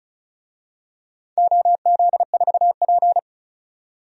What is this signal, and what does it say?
Morse code, heard as a keyed tone, about 35 wpm, 700 Hz. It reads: OZ4P
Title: DXCC-100 Call Signs - 4 Characters - 35wpm